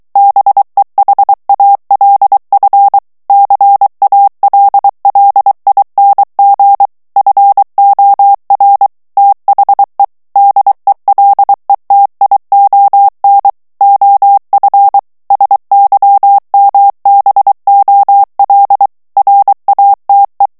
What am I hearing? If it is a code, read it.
Morse code, heard as a keyed tone, about 23 wpm, 800 Hz. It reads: BEHALF CALLING FOR THE DELETION OF SYMBOL RATE
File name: ANGLAIS texte clair 3 vitesse 22 mots minute